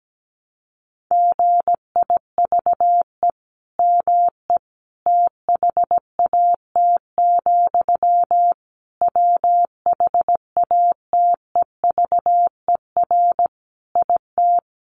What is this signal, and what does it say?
Morse code, heard as a keyed tone, about 17 wpm, 700 Hz. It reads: GIVE ME THAT, WHATEVER IT
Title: Sentences from Top 1100 Words 17wpm